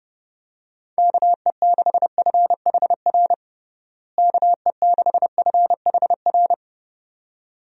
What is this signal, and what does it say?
Morse code, heard as a keyed tone, about 32 wpm, 700 Hz. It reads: KE6FHR KE6FHR
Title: USA Call Signs - 6 Characters - Repeated 2x - 30wpm